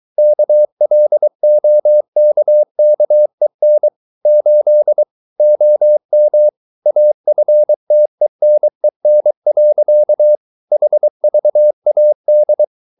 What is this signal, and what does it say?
Morse code, heard as a keyed tone, about 23 wpm, 600 Hz. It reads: KLOKKEN 8 OM AFTENEN. HVAD
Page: Ferie 23 wpm